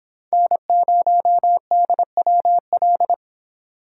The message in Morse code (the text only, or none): N0DWL